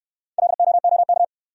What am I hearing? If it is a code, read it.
Morse code, heard as a keyed tone, about 62 wpm, 700 Hz. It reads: LOOK